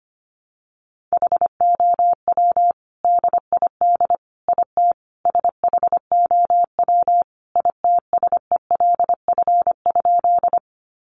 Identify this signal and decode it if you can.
Morse code, heard as a keyed tone, about 25 wpm, 700 Hz. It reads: HOW DID IT SHOW ITSELF?
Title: Sentences from Top 800 Words 25wpm